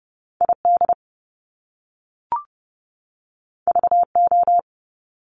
Morse code, heard as a keyed tone, about 30 wpm, 700 Hz. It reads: ID VO
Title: Top Two Letter Combinations in Words 30wpm